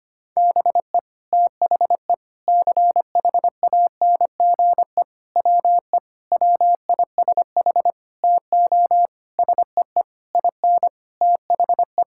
Morse code, heard as a keyed tone, about 25 wpm, 700 Hz. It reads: BE THE CHANGE WE WISH TO SEE IN THE
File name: practice / 25wpm / Sentences with 9-Word Phrases - 25wpm